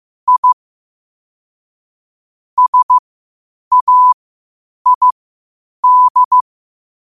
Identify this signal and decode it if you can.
Morse code, heard as a keyed tone, about 15 wpm, spaced as for 8 wpm, 1000 Hz. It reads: I SAID
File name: qotdfiles / 8wpm